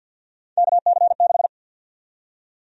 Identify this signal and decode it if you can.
Morse code, heard as a keyed tone, about 53 wpm, 700 Hz. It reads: KC6
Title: USA Call Sign Prefixes - 50wpm